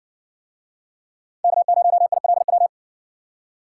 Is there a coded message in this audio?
Yes: Morse code from 1.4 to 2.7 s, heard as a keyed tone, about 60 wpm, 700 Hz. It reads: K0ILW